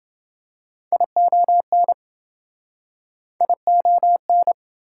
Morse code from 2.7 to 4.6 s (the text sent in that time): ION